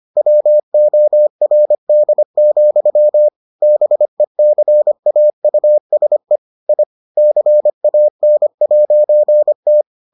Morse code, heard as a keyed tone, about 25 wpm, 600 Hz. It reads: WORD, BECAUSE I CAN'T